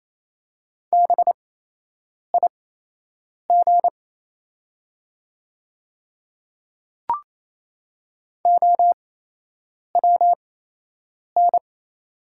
Morse code, heard as a keyed tone, about 28 wpm, spaced as for 8 wpm, 700 Hz. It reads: BIG OWN
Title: 8x Character Spacing - Top 300 Words - 28wpm